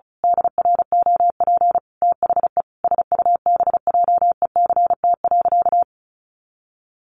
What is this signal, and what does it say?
Morse code, heard as a keyed tone, about 34 wpm, 700 Hz. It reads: DROP THE SUBJECT.
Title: Sentences from Top 1400 Words 35wpm